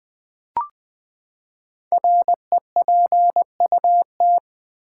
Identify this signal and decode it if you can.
Morse code, heard as a keyed tone, about 20 wpm, 700 Hz. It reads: REPUT
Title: Build A Word - 5 Letters At A Time - 10-Letter Words - 20wpm